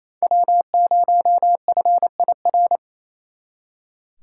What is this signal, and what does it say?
Morse code, heard as a keyed tone, about 35 wpm, 700 Hz. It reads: W0FIR
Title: Speed Racing - USA 1x3 Call Signs 28wpm